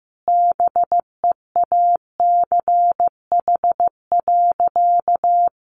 Morse code, heard as a keyed tone, about 15 wpm, 700 Hz. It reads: BEACH.
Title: Sentences from Top 1500 Words 15wpm